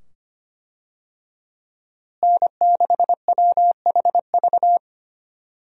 Morse code, heard as a keyed tone, about 25 wpm, 700 Hz. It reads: N6WHV
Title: USA Call Signs - 5 Characters - Repeated 2x - 25wpm